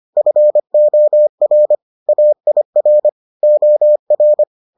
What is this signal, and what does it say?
Morse code, heard as a keyed tone, about 25 wpm, 600 Hz. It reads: FOR AIR OR